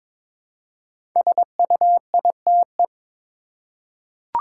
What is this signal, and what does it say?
Morse code, heard as a keyed tone, about 22 wpm, 690 Hz. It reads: SUITE
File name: 5 Letter Words - First 3 Letters Given - 22wpm